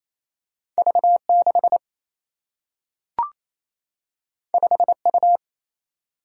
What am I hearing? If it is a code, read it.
Morse code, heard as a keyed tone, about 28 wpm, 700 Hz. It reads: V 6 5 U
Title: Single-Letter Single-Number 28wpm